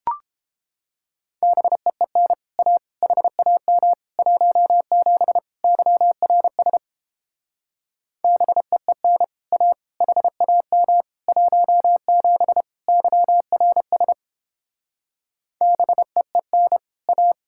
Morse code, heard as a keyed tone, about 29 wpm, 700 Hz. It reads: BEEN A HAM 17 YRS BEEN A HAM 17 YRS BEEN A